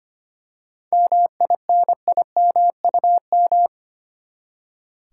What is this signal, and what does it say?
Morse code, heard as a keyed tone, about 25 wpm, 700 Hz. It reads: MINIMUM
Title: Seven Letter Words - 25wpm